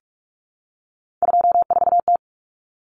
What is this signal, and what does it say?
Morse code, heard as a keyed tone, about 44 wpm, 700 Hz. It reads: 24T
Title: Common Cut Numbers - 3-Digit 45wpm